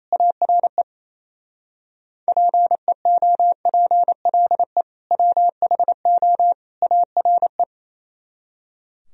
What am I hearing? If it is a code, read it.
Morse code, heard as a keyed tone, about 36 wpm, 700 Hz. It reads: ARE PEOPLE WHO ARE